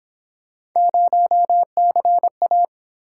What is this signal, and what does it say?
Morse code, heard as a keyed tone, about 27 wpm, 700 Hz. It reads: 0CA
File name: Speed-Racing - Three Letter-Number Combinations - 22wpm